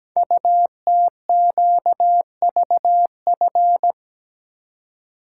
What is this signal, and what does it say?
Morse code, heard as a keyed tone, about 17 wpm, 700 Hz. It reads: UTQVF
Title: Code Groups - 5 Letters - 17wpm